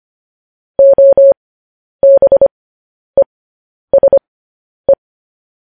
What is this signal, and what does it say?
Morse code, heard as a keyed tone, about 25 wpm, spaced as for 10 wpm, 550 Hz. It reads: OBESE